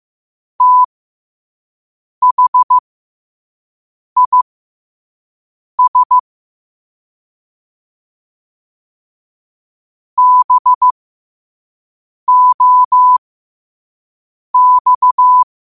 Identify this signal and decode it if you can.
Morse code, heard as a keyed tone, about 15 wpm, spaced as for 5 wpm, 1000 Hz. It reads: THIS BOX